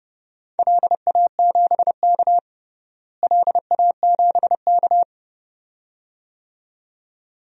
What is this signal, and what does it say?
Morse code, heard as a keyed tone, about 30 wpm, 700 Hz. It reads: LA7K LA7K
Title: DXCC-100 Call Signs - 4 Characters - Repeated 2x - 30wpm